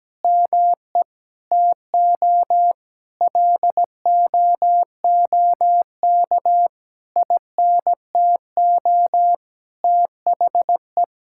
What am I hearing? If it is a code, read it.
Morse code, heard as a keyed tone, about 17 wpm, 700 Hz. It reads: ME TO LOOK INTO THE
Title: Sentences from Top 1000 Words - Review Part 6 - 17wpm